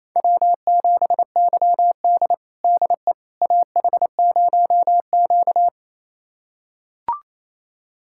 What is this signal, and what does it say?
Morse code, heard as a keyed tone, about 28 wpm, 700 Hz. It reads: W7YD DE AH0Q
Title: USA Callsign DE Callsign - 4 Characters - 28wpm